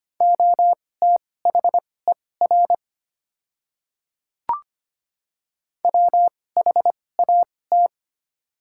Morse code, heard as a keyed tone, about 25 wpm, spaced as for 18 wpm, 700 Hz. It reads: OTHER WHAT
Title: 2x Character Spacing - Top 100 Words - 25wpm